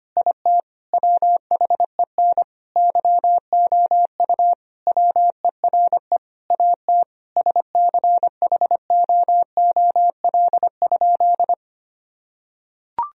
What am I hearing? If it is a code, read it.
Morse code, heard as a keyed tone, about 25 wpm, 700 Hz. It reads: IT WHEN YOU WERE AT SCHOOL?
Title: Sentences from Top 1000 Words - Review Part 1 - 25wpm